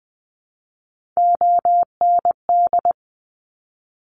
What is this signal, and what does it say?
Morse code, heard as a keyed tone, about 20 wpm, 700 Hz. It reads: OND
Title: Top Three Letter Combinations in Words 20wpm